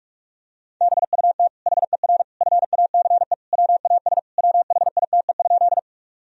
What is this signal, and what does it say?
Morse code, heard as a keyed tone, about 42 wpm, 700 Hz. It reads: BUT HER FACE WAS WHITE?
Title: Sentences from Top 1000 Words - Review Part 2 - 45wpm